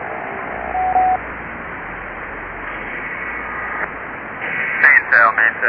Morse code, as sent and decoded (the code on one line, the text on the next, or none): none